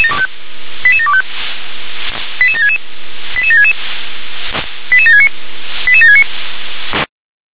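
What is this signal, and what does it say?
Five-tone selective call (ZVEI), heard as quick runs of five tones, tones of about 70 ms. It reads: 88725 88766 88766 88769 88769
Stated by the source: ZVEI five-tone selective call received by SDR.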